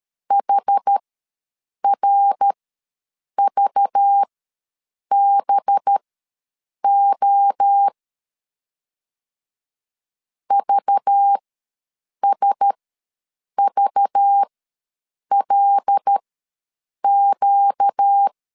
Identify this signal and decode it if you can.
Morse code, heard as a keyed tone, about 13 wpm, spaced as for 7 wpm, 800 Hz. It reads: HRVBO VSVLQ
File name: LEZIONI / 5 LBVC / 5 LBV ws 12wpm 3015